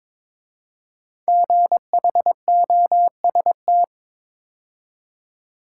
Morse code, heard as a keyed tone, about 22 wpm, 700 Hz. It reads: GHOST
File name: 5 Letter Words - First 3 Letters Given - 22wpm